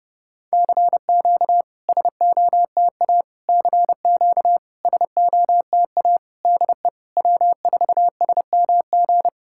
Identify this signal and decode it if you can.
Morse code, heard as a keyed tone, about 32 wpm, 700 Hz. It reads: CQ SOTA CQ SOTA DE W4SMG
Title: SOTA QSO Practice 30wpm